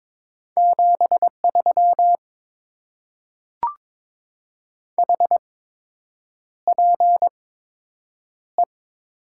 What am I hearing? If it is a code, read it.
Morse code, heard as a keyed tone, about 22 wpm, spaced as for 8 wpm, 700 Hz. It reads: 73 HPE
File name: Top 90 QSO Elements - 8x Character Spacing - 22wpm